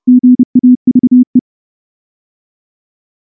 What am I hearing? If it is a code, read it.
Morse code, heard as a keyed tone, about 30 wpm, 250 Hz. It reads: GAVE
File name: morse-training-30WPM